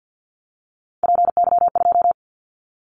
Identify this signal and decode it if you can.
Morse code, heard as a keyed tone, about 50 wpm, 700 Hz. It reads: PY2
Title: Callsign Prefixes for Top 50 DXCC - 50wpm